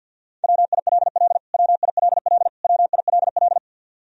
Morse code, heard as a keyed tone, about 53 wpm, 700 Hz. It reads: WILL WILL WILL
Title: Top 100 Words Repeated 50wpm